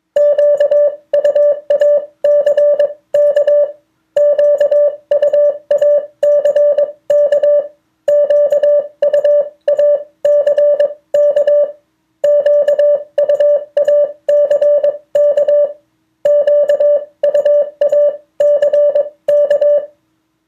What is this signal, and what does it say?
Morse code, heard as a keyed tone, about 21 wpm, spaced as for 18 wpm, 580 Hz. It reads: QUACK QUACK QUACK QUACK QUACK
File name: 2of4-words-1